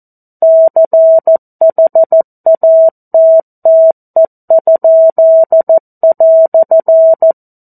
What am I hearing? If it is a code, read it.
Morse code, heard as a keyed tone, about 14 wpm, 650 Hz. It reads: CHATTE?"
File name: 02-Chapter-02 14 WPM